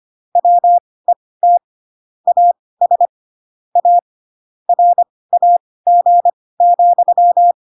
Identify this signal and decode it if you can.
Morse code, heard as a keyed tone, about 25 wpm, spaced as for 18 wpm, 700 Hz. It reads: WET AS A RAG,
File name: Pinocchio / Pinocchio ch6 12 25 ik1hge